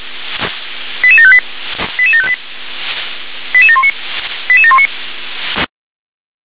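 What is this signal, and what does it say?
Five-tone selective call (ZVEI), heard as quick runs of five tones, tones of about 70 ms. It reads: 88758 88758 88719 88719
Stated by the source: ZVEI five-tone selective call received by SDR.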